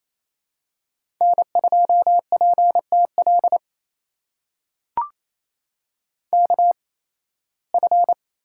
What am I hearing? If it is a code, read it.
Morse code, heard as a keyed tone, about 28 wpm, spaced as for 18 wpm, 700 Hz. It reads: N2PTL KF
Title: USA Call Signs - 5 Characters - 8x Character Spacing - 28wpm